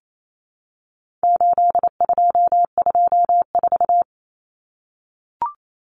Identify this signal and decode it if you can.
Morse code, heard as a keyed tone, about 28 wpm, 700 Hz. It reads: 8224
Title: Common Cut Numbers - 4-Digit 28wpm